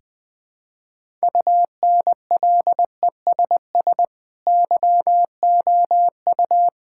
Morse code, heard as a keyed tone, about 20 wpm, 700 Hz. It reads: UNLESS YOU
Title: Sentences from Top 1000 Words - Review Part 6 - 20wpm